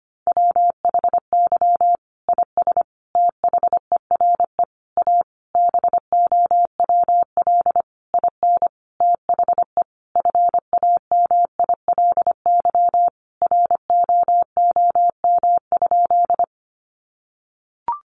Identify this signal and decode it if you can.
Morse code, heard as a keyed tone, about 25 wpm, 700 Hz. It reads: WHY IS THERE A BOWL IN THE FAMILY ROOM?
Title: Sentences from Top 1200 Words 25wpm